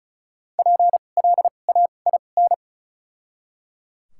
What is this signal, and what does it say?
Morse code, heard as a keyed tone, about 36 wpm, spaced as for 25 wpm, 700 Hz. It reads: PLAIN
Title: 2x Character Spacing - Top 500 Words - 35wpm